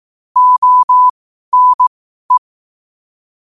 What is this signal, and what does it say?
Morse code, heard as a keyed tone, about 18 wpm, spaced as for 12 wpm, 1000 Hz. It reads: ONE
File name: qotdfiles / 12wpm